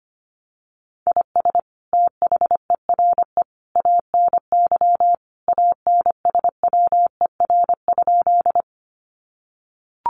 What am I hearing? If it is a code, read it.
Morse code, heard as a keyed tone, about 25 wpm, 700 Hz. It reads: IS THERE ANY ANSWER?
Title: Sentences from Top 300 Words 25wpm